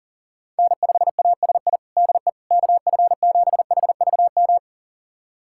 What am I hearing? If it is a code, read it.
Morse code, heard as a keyed tone, about 37 wpm, 700 Hz. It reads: N5ASI DE KF7HVM
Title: USA Callsign DE Callsign 40wpm